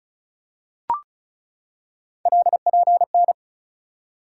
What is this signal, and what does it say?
Morse code, heard as a keyed tone, about 36 wpm, 700 Hz. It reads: LPN
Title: Practice QSO - Exchange 3 - More Info - 35wpm